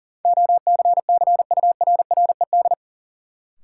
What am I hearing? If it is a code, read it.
Morse code, heard as a keyed tone, about 40 wpm, 700 Hz. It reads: OCCURRED
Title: Sets of 5 Words Encore - New Word Drill - 40wpm